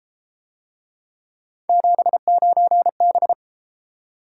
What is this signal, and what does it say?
Morse code, heard as a keyed tone, about 33 wpm, 700 Hz. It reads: 79B